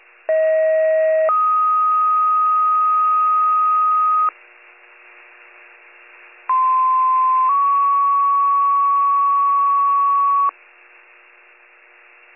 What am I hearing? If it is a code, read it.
Two-tone page, heard as a one-second tone, then a three-second tone, about 640 then 1250 Hz, about 1040 then 1120 Hz. It is heard from 0.3 to 10.5 s.